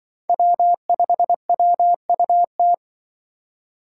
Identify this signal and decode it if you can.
Morse code, heard as a keyed tone, about 24 wpm, 700 Hz. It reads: W5WUT
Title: Speed Racing - USA 1x3 Call Signs 20wpm